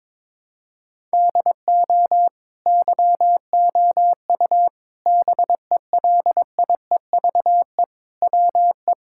Morse code, heard as a keyed tone, about 21 wpm, 700 Hz. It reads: DO YOU BELIEVE WE
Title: Sentences from Top 1000 Words - Review Part 6 - 22wpm